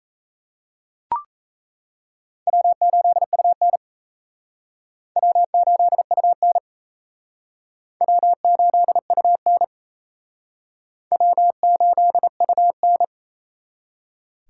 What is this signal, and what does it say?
Morse code, heard as a keyed tone, about 34 wpm, 700 Hz. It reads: W8UN W8UN W8UN W8UN
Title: Speed Racing - USA Call Signs 28wpm